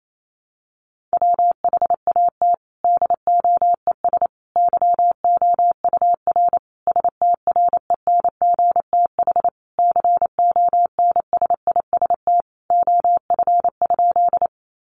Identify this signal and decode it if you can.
Morse code, heard as a keyed tone, about 28 wpm, 700 Hz. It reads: WHAT DOES YOUR STRENGTH CONSIST OF?
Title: Sentences from Top 1400 Words 28wpm